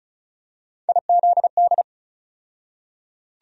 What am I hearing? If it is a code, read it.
Morse code, heard as a keyed tone, about 36 wpm, 700 Hz. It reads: IZD
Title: USA Call Sign Postfixes - 3 Letters 35wpm